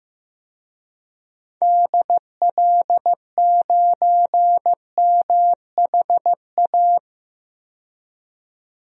Morse code, heard as a keyed tone, about 15 wpm, 700 Hz. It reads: DL9MHA
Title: DXCC-100 Call Signs - 6 Characters - 15wpm